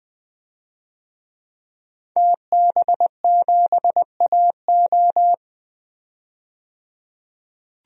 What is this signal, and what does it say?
Morse code, heard as a keyed tone, about 20 wpm, 700 Hz. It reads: TB7AO
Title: DXCC-100 Call Signs - 5 Characters - 20wpm